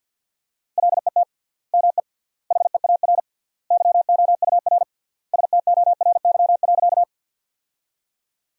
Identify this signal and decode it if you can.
Morse code, heard as a keyed tone, about 50 wpm, 700 Hz. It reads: LET ME HEAR YOUR STORY.
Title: 3x Word-Spacing - Sentences from Top 300 Words - 50wpm